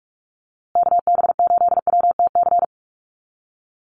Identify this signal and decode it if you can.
Morse code, heard as a keyed tone, about 44 wpm, 700 Hz. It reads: KB8WTC
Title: USA Call Signs Repeated 2x 45wpm